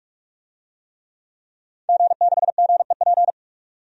Morse code, heard as a keyed tone, about 46 wpm, 700 Hz. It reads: G6GEP
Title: DXCC-100 Call Signs - 5 Characters - 45wpm